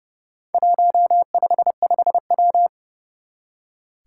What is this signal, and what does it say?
Morse code, heard as a keyed tone, about 30 wpm, 700 Hz. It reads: 155W
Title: Four Challenging Letter-Number Combinations 30wpm